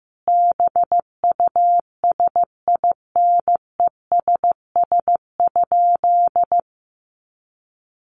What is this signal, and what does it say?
Morse code, heard as a keyed tone, about 15 wpm, 700 Hz. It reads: BUSINESS?
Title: Sentences from Top 600 Words 15wpm